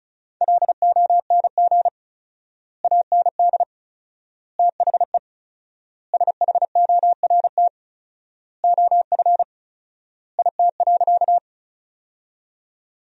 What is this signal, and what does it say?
Morse code, heard as a keyed tone, about 34 wpm, 700 Hz. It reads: LONG AND THE SHORT OF IT.
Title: 4x Word-Spacing - Sentences from Top 400 Words - 35wpm